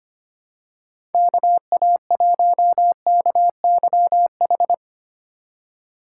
Morse code, heard as a keyed tone, about 25 wpm, 700 Hz. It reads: KA1KYH
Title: USA Call Signs - 6 Characters - 3x Character Spacing - 25wpm